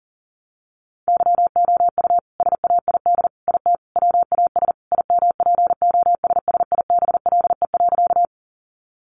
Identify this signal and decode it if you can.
Morse code, heard as a keyed tone, about 40 wpm, 700 Hz. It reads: YOU SAID IT WAS IMPOSSIBLE.
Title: Sentences from Top 1100 Words 40wpm